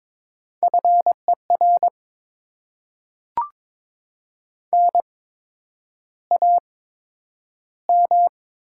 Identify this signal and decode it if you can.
Morse code, heard as a keyed tone, about 22 wpm, spaced as for 9 wpm, 700 Hz. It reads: FER NAM
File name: Top 20 QSO Elements - 8x Character Spacing - 22wpm